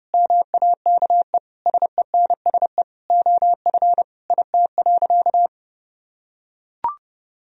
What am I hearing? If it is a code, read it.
Morse code, heard as a keyed tone, about 32 wpm, 700 Hz. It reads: MAKE SENSE OF IT.
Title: Sentences from Top 1000 Words - Review Part 4 - 30wpm